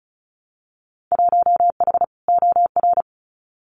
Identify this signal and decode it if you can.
Morse code, heard as a keyed tone, about 36 wpm, 700 Hz. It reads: 1H OR
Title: Winter Field Day Exchanges Encore 35wpm